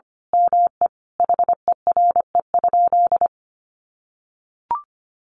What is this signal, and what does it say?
Morse code, heard as a keyed tone, about 25 wpm, 700 Hz. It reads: ME HERE?
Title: Sentences from Top 1200 Words 25wpm